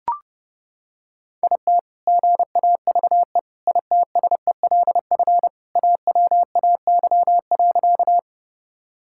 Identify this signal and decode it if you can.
Morse code, heard as a keyed tone, about 30 wpm, 700 Hz. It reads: IT GAVE ITSELF AWAY.
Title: Sentences from Top 1000 Words - Review Part 5 - 30wpm